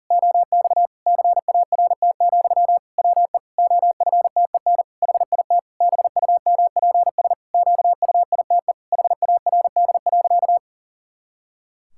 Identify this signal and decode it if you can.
Morse code, heard as a keyed tone, about 40 wpm, 700 Hz. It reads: OX CART, WE OFTEN HIT BUMPS QUITE HARD.
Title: Pangram Challenge v2 - 40wpm